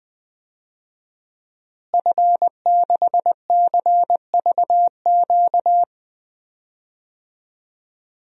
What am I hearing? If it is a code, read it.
Morse code, heard as a keyed tone, about 20 wpm, 700 Hz. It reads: F6CVQ
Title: DXCC-100 Call Signs - 5 Characters - Repeated 3x - 20wpm